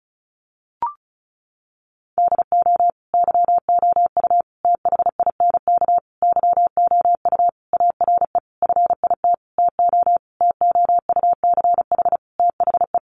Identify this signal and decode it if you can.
Morse code, heard as a keyed tone, about 34 wpm, 700 Hz. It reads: DO YOU THINK YOU ARE FIT TO TOUCH THE
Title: Sentences from Top 1200 Words 35wpm